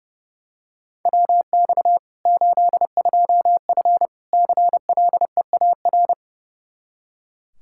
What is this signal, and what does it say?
Morse code, heard as a keyed tone, about 30 wpm, 700 Hz. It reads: WX 82F CLEAR
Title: Practice QSO - Exchange 2 - Info - 30wpm